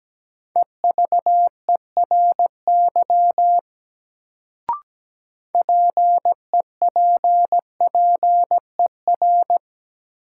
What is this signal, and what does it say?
Morse code, heard as a keyed tone, about 17 wpm, 700 Hz. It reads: EVERY PEPPER
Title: Sets of 5 Words - New Word Drill - 17wpm